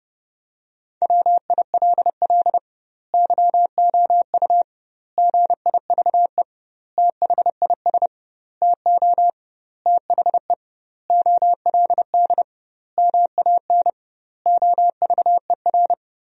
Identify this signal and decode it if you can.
Morse code, heard as a keyed tone, about 30 wpm, 700 Hz. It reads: WILL YOU GIVE THIS TO THE OLD MAN OVER
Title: 2x Word-Spacing - Sentences from Top 200 Words - 30wpm